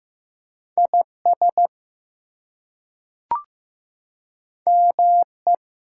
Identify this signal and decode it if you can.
Morse code, heard as a keyed tone, about 15 wpm, 700 Hz. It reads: IS ME
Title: Two Letter Words - 15wpm